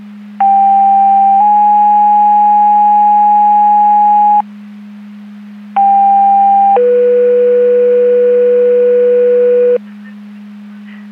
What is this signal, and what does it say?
Two-tone page, heard as a one-second tone, then a three-second tone, about 790 then 830 Hz, about 800 then 490 Hz. It is heard from 0.4 to 9.8 s.
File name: Portage County 154.3850 Fire EMS Paging